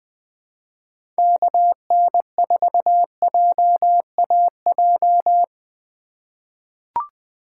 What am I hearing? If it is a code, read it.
Morse code, heard as a keyed tone, about 20 wpm, 700 Hz. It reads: KN4JAJ
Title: USA Call Signs - 6 Characters - 4x Character Spacing - 20wpm